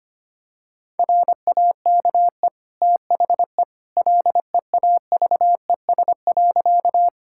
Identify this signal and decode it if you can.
Morse code, heard as a keyed tone, about 25 wpm, 700 Hz. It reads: RAKE THE LEAVES.
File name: Easy Sentences with Hints - 25wpm